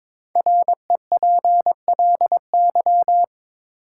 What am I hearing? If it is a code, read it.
Morse code, heard as a keyed tone, about 22 wpm, 690 Hz. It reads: REPLY
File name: 5 Letter Words - First 3 Letters Given - 22wpm